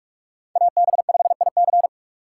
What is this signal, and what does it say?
Morse code, heard as a keyed tone, about 44 wpm, 700 Hz. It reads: AB5IC